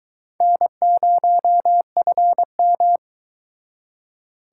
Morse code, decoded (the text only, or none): N0FM